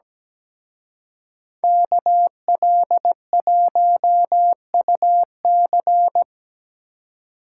Morse code, heard as a keyed tone, about 20 wpm, 700 Hz. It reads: KL1UC
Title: Speed Racing - USA 2x2 Call Signs 17wpm